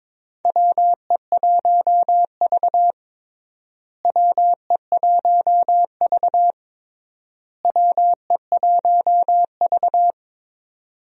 Morse code, heard as a keyed tone, about 22 wpm, 700 Hz. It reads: WE1V WE1V WE1V
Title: USA Call Signs - 4 Characters - Repeated 3x - 22wpm